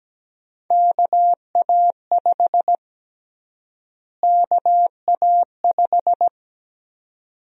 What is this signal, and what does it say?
Morse code, heard as a keyed tone, about 17 wpm, 700 Hz. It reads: KA5 KA5
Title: USA Call Sign Prefixes - Repeated 2x - 17wpm